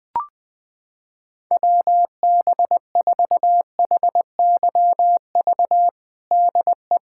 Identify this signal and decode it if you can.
Morse code, heard as a keyed tone, about 20 wpm, 700 Hz. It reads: WB4HYV DE
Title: USA Callsign DE Callsign - 6 Characters - 20wpm